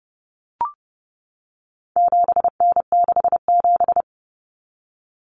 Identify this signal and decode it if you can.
Morse code, heard as a keyed tone, about 30 wpm, 700 Hz. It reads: 7N67
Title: Common Cut Numbers - 4-Digit 30wpm